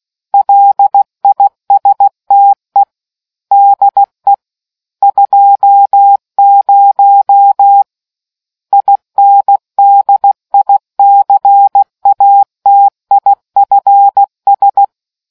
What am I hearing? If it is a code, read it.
Morse code, heard as a keyed tone, about 16 wpm, 800 Hz. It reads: LISTE DE 20 INDICATIFS